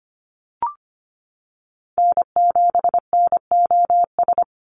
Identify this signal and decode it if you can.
Morse code, heard as a keyed tone, about 25 wpm, 700 Hz. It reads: N7NOS
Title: USA Call Signs 25wpm